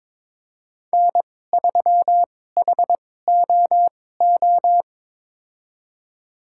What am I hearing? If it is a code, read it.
Morse code, heard as a keyed tone, about 22 wpm, spaced as for 16 wpm, 700 Hz. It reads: N3HOO
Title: USA Call Signs - 5 Characters - 2x Character Spacing - 22wpm